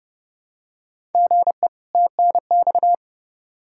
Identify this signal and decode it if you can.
Morse code, heard as a keyed tone, about 31 wpm, 700 Hz. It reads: GE TNX
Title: Practice QSO - Exchange 1 - Beginning - 30wpm